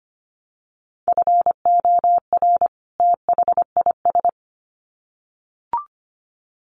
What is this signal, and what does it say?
Morse code, heard as a keyed tone, about 25 wpm, 700 Hz. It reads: FOR THIS
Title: Sets of 2 Words 25wpm